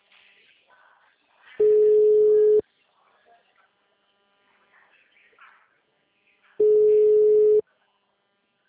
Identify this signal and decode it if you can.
Ringback tone (Europe, 425 Hz).